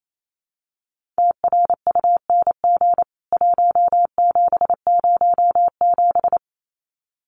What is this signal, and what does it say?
Morse code, heard as a keyed tone, about 28 wpm, 700 Hz. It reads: TRUNG 1707
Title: CWT Exchange 28wpm